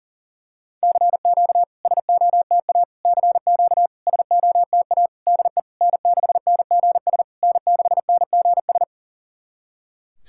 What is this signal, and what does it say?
Morse code, heard as a keyed tone, about 40 wpm, 700 Hz. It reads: CQ SOTA CQ SOTA DE N6NGS N6NGS
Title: SOTA QSO Practice 40wpm